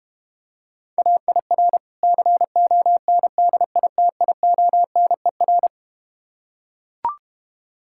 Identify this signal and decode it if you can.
Morse code, heard as a keyed tone, about 32 wpm, 700 Hz. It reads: AIR CONDITIONER